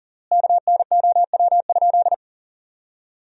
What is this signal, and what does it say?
Morse code, heard as a keyed tone, about 40 wpm, 700 Hz. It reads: KNOW?
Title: 3x Word-Spacing - Sentences from Top 200 Words - 40wpm